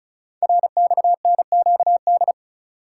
Morse code, heard as a keyed tone, about 36 wpm, 700 Hz. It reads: RXNQD
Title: Code Groups - 5 Letters - 35wpm